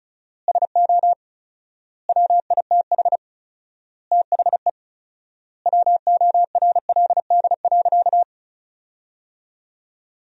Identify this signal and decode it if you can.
Morse code, heard as a keyed tone, about 34 wpm, 700 Hz. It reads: SO WITH THE WORLD.
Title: 4x Word-Spacing - Sentences from Top 200 Words - 35wpm